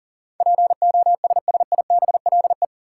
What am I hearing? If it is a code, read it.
Morse code, heard as a keyed tone, about 40 wpm, 700 Hz. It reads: POSSIBLE
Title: Sentences from Top 500 Words - New Word Drill - 40wpm